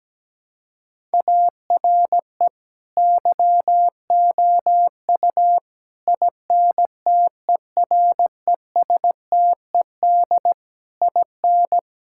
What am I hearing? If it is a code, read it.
Morse code, heard as a keyed tone, about 17 wpm, 700 Hz. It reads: ARE YOU INTERESTED IN
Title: Sentences from Top 1000 Words - Review Part 6 - 17wpm